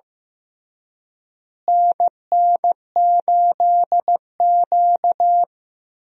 Morse code, heard as a keyed tone, about 15 wpm, 700 Hz. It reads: NN8Q
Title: USA Call Signs - 4 Characters - Repeated 2x - 15wpm